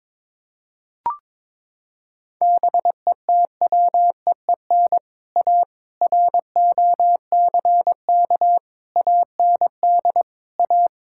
Morse code, heard as a keyed tone, about 23 wpm, 700 Hz. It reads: BETWEEN A ROCK AND A
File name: practice / 22wpm / Sets of 7 Words - 22wpm